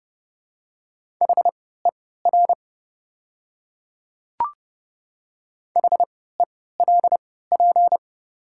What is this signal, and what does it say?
Morse code, heard as a keyed tone, about 30 wpm, spaced as for 17 wpm, 700 Hz. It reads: HER HELP
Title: 3x Character Spacing - Top 500 Words - 30wpm